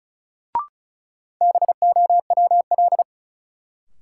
Morse code, heard as a keyed tone, about 34 wpm, 700 Hz. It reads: BOWL
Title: Sentences from Top 1200 Words - New Word Drill - 35wpm